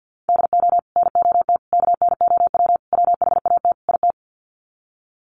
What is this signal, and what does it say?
Morse code, heard as a keyed tone, about 50 wpm, 700 Hz. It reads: DO NOT KNOW WHAT IT
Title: Sets of 5 Words 50wpm